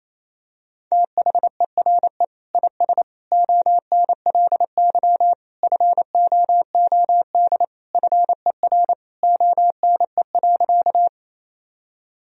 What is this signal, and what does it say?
Morse code, heard as a keyed tone, about 28 wpm, 700 Hz. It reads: THERE IS ONLY FOOD FER ONE.
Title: Sentences from Top 300 Words and Abbreviations 28wpm